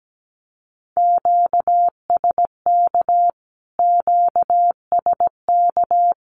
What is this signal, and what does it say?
Morse code, heard as a keyed tone, about 17 wpm, 700 Hz. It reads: QSK QSK
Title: Q-Codes Repeated 3x 17wpm